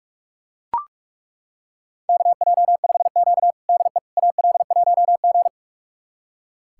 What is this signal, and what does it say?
Morse code, heard as a keyed tone, about 41 wpm, 700 Hz. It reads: KJ5Q DE AL1G
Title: USA Callsign DE Callsign - 4 Characters - 45wpm